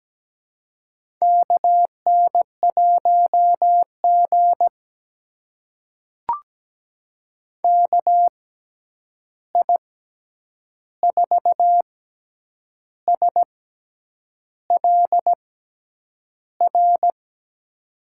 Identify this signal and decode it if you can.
Morse code, heard as a keyed tone, about 17 wpm, spaced as for 8 wpm, 700 Hz. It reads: KN1G KI4SLR
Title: USA Call Signs - 6x Character Spacing - 17wpm